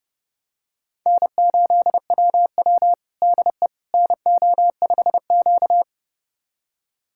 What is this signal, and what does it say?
Morse code, heard as a keyed tone, about 31 wpm, 700 Hz. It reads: N8WW DE NO5Q
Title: USA Callsign DE Callsign - 4 Characters - 30wpm